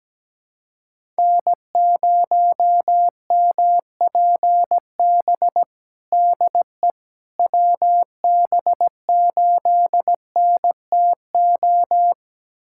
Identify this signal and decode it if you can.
Morse code, heard as a keyed tone, about 17 wpm, 700 Hz. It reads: N0MPB DE WB8NTO
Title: USA Callsign DE Callsign 17wpm